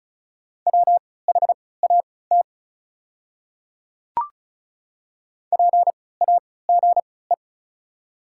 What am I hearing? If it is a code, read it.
Morse code, heard as a keyed tone, about 36 wpm, spaced as for 20 wpm, 700 Hz. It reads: WHAT PAGE
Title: 3x Character Spacing - Top 300 Words - 35wpm